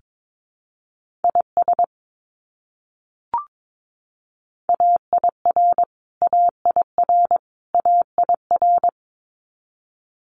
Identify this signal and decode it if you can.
Morse code, heard as a keyed tone, about 22 wpm, 700 Hz. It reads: IS AIR AIR AIR
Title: Top 300 Words Repeated 22wpm